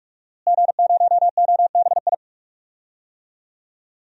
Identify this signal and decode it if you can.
Morse code, heard as a keyed tone, about 46 wpm, 700 Hz. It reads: G0OBI
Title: DXCC-100 Call Signs - 5 Characters - Repeated 2x - 45wpm